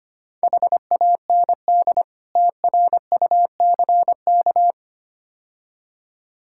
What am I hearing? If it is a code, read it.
Morse code, heard as a keyed tone, about 25 wpm, 700 Hz. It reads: HAND TRUCK
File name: practice / 25wpm / Difficult Words - Speed-Racing with Hints - 25wpm